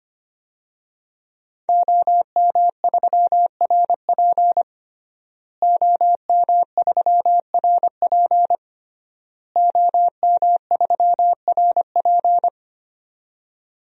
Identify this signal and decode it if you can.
Morse code, heard as a keyed tone, about 25 wpm, 700 Hz. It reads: OM3RP OM3RP OM3RP
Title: DXCC-100 Call Signs - 5 Characters - Repeated 3x - 25wpm